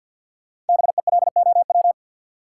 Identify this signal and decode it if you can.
Morse code, heard as a keyed tone, about 53 wpm, 700 Hz. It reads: BELOW
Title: Sentences from Top 1200 Words - New Word Drill Alt - 50wpm